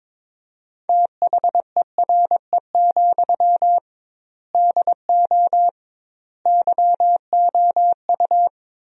Morse code, heard as a keyed tone, about 22 wpm, 700 Hz. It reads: THERE, DO YOU
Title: 2x Word-Spacing - Sentences from Top 300 Words - 22wpm